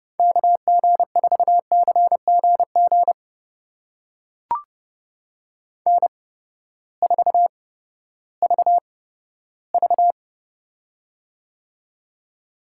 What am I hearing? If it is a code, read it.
Morse code, heard as a keyed tone, about 32 wpm, spaced as for 15 wpm, 700 Hz. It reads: KG4CGG N4VV
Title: USA Call Signs - 8x Character Spacing - 30wpm